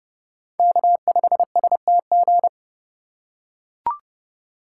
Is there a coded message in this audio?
Yes: Morse code from 0.6 to 2.5 s, heard as a keyed tone, about 29 wpm, 700 Hz. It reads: K5STG